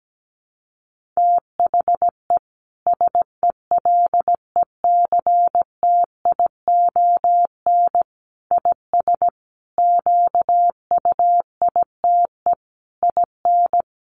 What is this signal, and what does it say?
Morse code, heard as a keyed tone, about 17 wpm, 700 Hz. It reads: THE SELECTION IS QUITE IN
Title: Sentences from Top 1400 Words 17wpm